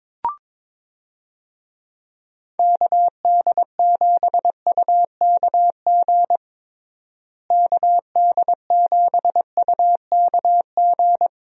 Morse code, heard as a keyed tone, about 22 wpm, 700 Hz. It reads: KD7UKG KD7UKG
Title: USA Call Signs - 6 Characters - Repeated 2x - 22wpm